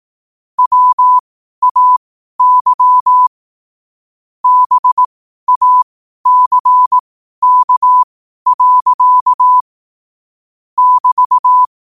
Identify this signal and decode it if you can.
Morse code, heard as a keyed tone, about 18 wpm, spaced as for 12 wpm, 1000 Hz. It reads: WAY BACK. <BT>
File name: qotdfiles / 12wpm